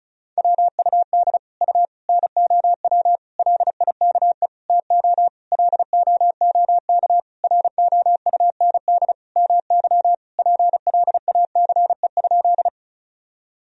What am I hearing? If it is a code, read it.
Morse code, heard as a keyed tone, about 34 wpm, 690 Hz. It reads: WUD U NOW LIKE TO LOOK ROUND MY PLACE?
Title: Sentences from Top 200 Words and Abbreviations 35wpm